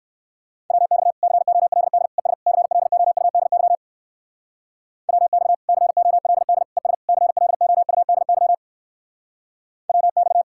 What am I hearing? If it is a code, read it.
Morse code, heard as a keyed tone, about 57 wpm, 700 Hz. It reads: WX COLD ES CLOUDY WX COLD ES CLOUDY WX